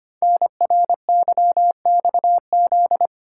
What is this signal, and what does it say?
Morse code, heard as a keyed tone, about 25 wpm, 700 Hz. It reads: NRYXZ
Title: Code Groups - 5 Letters - 25wpm